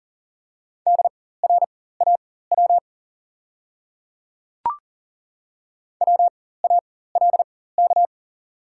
Morse code, heard as a keyed tone, about 40 wpm, spaced as for 19 wpm, 700 Hz. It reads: DRAW WALK
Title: 4x Character Spacing - Top 300 Words - 40wpm